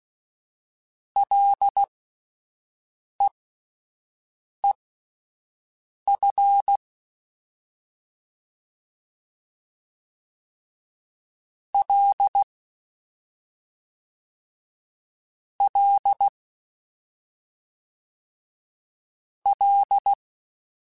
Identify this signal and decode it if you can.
Morse code, heard as a keyed tone, about 16 wpm, spaced as for 5 wpm, 800 Hz. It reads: LEEF L L L
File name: Ls / Single L m3s / Single L Prefixe 3x2 Twice @5